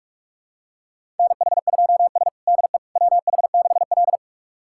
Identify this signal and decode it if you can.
Morse code, heard as a keyed tone, about 44 wpm, 690 Hz. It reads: NH2S DE WH6L
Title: USA Callsign DE Callsign - 4 Characters - 45wpm